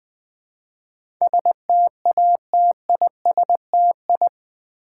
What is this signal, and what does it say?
Morse code, heard as a keyed tone, about 20 wpm, 700 Hz. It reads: STATISTI
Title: Build A Word - 4 Letters At A Time - 11-Letter Words - 20wpm